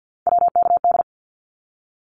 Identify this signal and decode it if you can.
Morse code, heard as a keyed tone, about 50 wpm, 700 Hz. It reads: WKD
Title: Abbreviations Repeated 3x - 50wpm